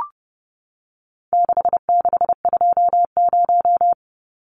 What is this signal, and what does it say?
Morse code, heard as a keyed tone, about 30 wpm, 700 Hz. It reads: 6620
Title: Numbers - 4-Digit 30wpm